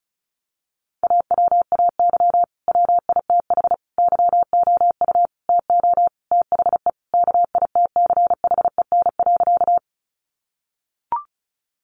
Morse code, heard as a keyed tone, about 34 wpm, 700 Hz. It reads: AWAY WITH YOU TO THE KITCHEN.
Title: Sentences from Top 1400 Words 35wpm